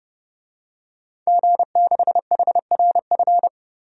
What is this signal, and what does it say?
Morse code, heard as a keyed tone, about 32 wpm, 700 Hz. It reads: G6HRF
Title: DXCC-100 Call Signs - 5 Characters - Repeated 2x - 30wpm